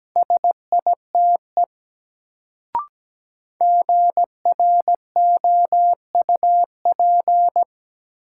Morse code, heard as a keyed tone, about 17 wpm, 700 Hz. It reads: SITE GROUP
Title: Sentences from Top 1000 Words - Review Part 1 - New Word Drill - 17wpm